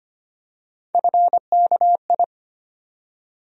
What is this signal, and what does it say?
Morse code, heard as a keyed tone, about 25 wpm, 700 Hz. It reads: FKI
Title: USA Call Sign Postfixes - 3 Letters 25wpm